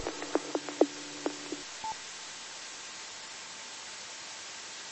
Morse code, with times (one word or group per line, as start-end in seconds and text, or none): none